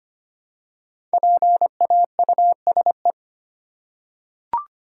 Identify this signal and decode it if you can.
Morse code, heard as a keyed tone, about 25 wpm, 700 Hz. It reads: PAUSE
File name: 5 Letter Words - First 3 Letters Given - 25wpm